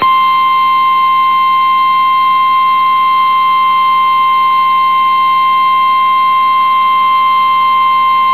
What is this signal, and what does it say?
Emergency Alert System (SAME) tones (the 1050 Hz attention tone). It is heard from 0.4 to 7.6 s.
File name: eas received alerts / L1 19555 audio